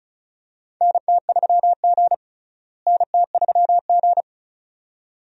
Morse code, heard as a keyed tone, about 36 wpm, 700 Hz. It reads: NT3G NT3G
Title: USA Call Signs - 4 Characters - Repeated 2x - 35wpm